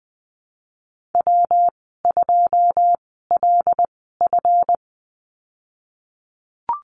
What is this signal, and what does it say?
Morse code, heard as a keyed tone, about 20 wpm, spaced as for 15 wpm, 700 Hz. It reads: W2LF
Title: 186 Lesson 30g - Letter b - Call Signs 20wpm